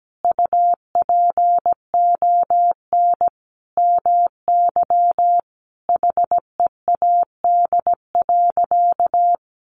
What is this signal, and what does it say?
Morse code, heard as a keyed tone, about 17 wpm, 700 Hz. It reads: UPON MY HEAD.
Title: Sentences from Top 1100 Words 17wpm